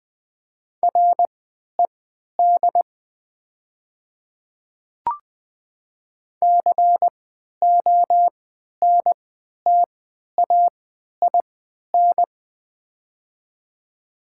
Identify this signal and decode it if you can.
Morse code, heard as a keyed tone, about 21 wpm, spaced as for 11 wpm, 700 Hz. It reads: RED CONTAIN
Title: 3x Character Spacing - Top 500 Words - 20wpm